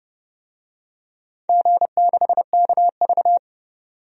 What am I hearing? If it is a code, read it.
Morse code, heard as a keyed tone, about 30 wpm, 700 Hz. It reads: G6KV
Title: DXCC-100 Call Signs - 4 Characters - 30wpm